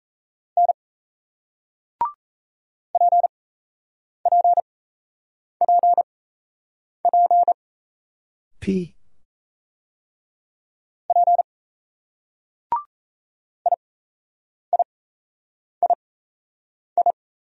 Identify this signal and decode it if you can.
Morse code, heard as a keyed tone, about 38 wpm, spaced as for 8 wpm, 700 Hz. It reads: N PPPP P IIII